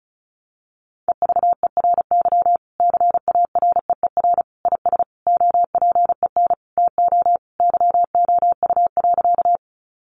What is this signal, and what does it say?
Morse code, heard as a keyed tone, about 34 wpm, 700 Hz. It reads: EVERY CAREER IS OPEN TO YOU.
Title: Sentences from Top 1000 Words 35wpm